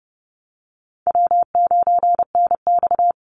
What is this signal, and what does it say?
Morse code, heard as a keyed tone, about 30 wpm, 700 Hz. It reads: W9NX
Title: Easy Call Signs - 30wpm